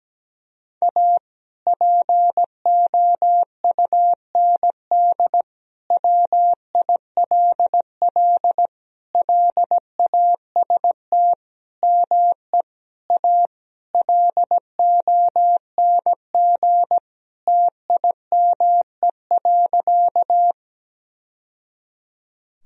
Morse code, heard as a keyed tone, about 17 wpm, 700 Hz. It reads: A POUND WILL LAST ME A LONG TIME.
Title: Sentences from Top 1000 Words - Review Part 2 - 17wpm